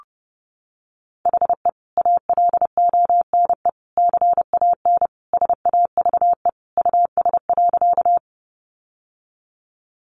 Morse code, heard as a keyed tone, about 30 wpm, 700 Hz. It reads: HE ALONE CAN SAVE US.
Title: Sentences from Top 1100 Words 30wpm